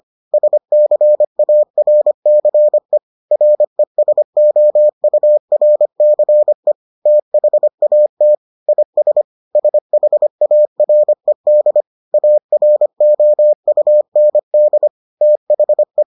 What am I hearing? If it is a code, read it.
Morse code, heard as a keyed tone, about 25 wpm, 600 Hz. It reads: SCARCE RESOURCE THAT IS SHARED AROUND THE